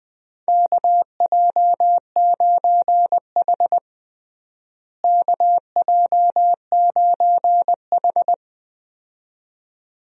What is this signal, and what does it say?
Morse code, heard as a keyed tone, about 20 wpm, 700 Hz. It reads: KJ9H KJ9H
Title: USA Call Signs - 4 Characters - Repeated 2x - 20wpm